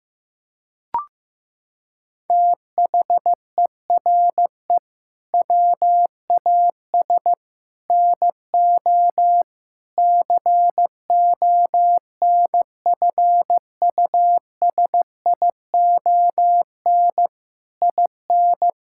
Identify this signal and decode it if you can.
Morse code, heard as a keyed tone, about 15 wpm, 700 Hz. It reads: THERE WAS NO CONFUSION IN
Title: Sentences from Top 1600 Words 15wpm